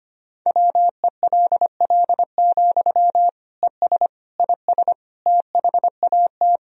Morse code, heard as a keyed tone, about 25 wpm, 700 Hz. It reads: WELL, ES IS THAT
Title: Sentences from Top 200 Words and Abbreviations 25wpm